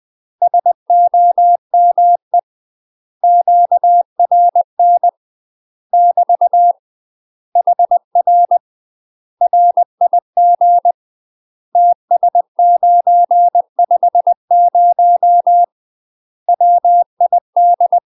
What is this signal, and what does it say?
Morse code, heard as a keyed tone, about 20 wpm, 700 Hz. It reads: SOME QRN <BT> HR RIG TS950 WID
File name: UK MASTER QSOs TRAINING / 20WPM / UK MASTER QSOs 0004